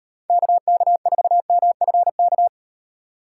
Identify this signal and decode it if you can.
Morse code, heard as a keyed tone, about 38 wpm, 700 Hz. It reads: KK4MFK